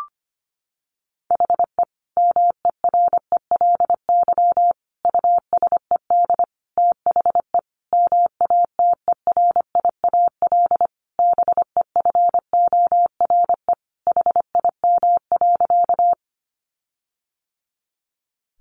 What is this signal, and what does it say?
Morse code, heard as a keyed tone, about 25 wpm, 700 Hz. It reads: HE MERELY USED THE MATERIAL BEFORE HIM.
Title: Sentences from Top 1400 Words 25wpm